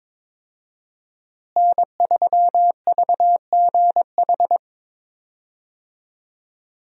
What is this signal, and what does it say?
Morse code, heard as a keyed tone, about 22 wpm, 690 Hz. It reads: N3VGH
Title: USA Call Signs - 5 Characters - Repeated 3x - 22wpm